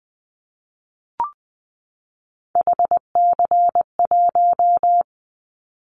Morse code, heard as a keyed tone, about 20 wpm, 700 Hz. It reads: HC1
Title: Callsign Prefixes for Top 100 DXCC - 20wpm